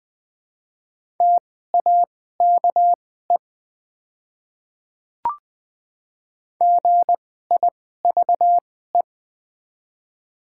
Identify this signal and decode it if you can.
Morse code, heard as a keyed tone, about 20 wpm, spaced as for 15 wpm, 700 Hz. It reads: TAKE GIVE
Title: 2x Character Spacing - Top 300 Words - 20wpm